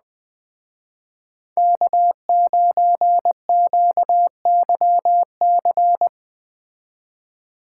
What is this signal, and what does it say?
Morse code, heard as a keyed tone, about 20 wpm, 700 Hz. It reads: K9QYC
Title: Speed Racing - USA 1x3 Call Signs 17wpm